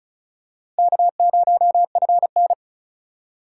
Morse code, heard as a keyed tone, about 36 wpm, 700 Hz. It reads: K0FN
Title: USA Call Signs - 6x Character Spacing - 35wpm